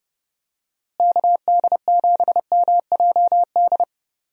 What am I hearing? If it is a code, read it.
Morse code, heard as a keyed tone, about 32 wpm, 700 Hz. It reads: KD7MJD
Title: USA Call Signs - 6 Characters - Repeated 3x - 30wpm